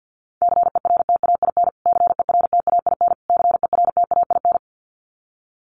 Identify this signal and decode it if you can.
Morse code, heard as a keyed tone, about 50 wpm, 700 Hz. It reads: CERTAIN CERTAIN CERTAIN
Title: Top 500 Words Repeated 50wpm